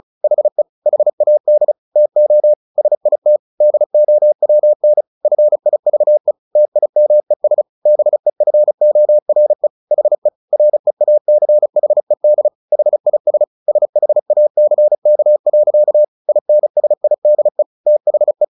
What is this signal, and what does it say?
Morse code, heard as a keyed tone, about 34 wpm, 600 Hz. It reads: HE HAD TO SIT DOWN FIVE TIMES BEFORE HE REACHED HIS SHACK. INSIDE THE
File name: the old man and the sea 2 35wpm